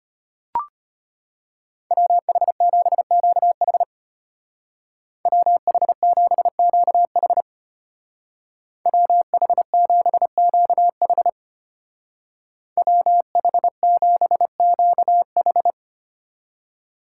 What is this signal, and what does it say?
Morse code, heard as a keyed tone, about 32 wpm, 700 Hz. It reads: WH7QH WH7QH WH7QH WH7QH